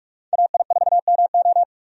Morse code, heard as a keyed tone, about 46 wpm, 700 Hz. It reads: AI4MO
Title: USA Call Signs - 5 Characters - 45wpm